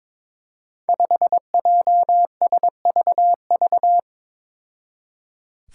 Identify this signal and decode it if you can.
Morse code, heard as a keyed tone, about 22 wpm, 700 Hz. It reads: 5JSVV
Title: Five Challenging Letter-Number Combinations 22wpm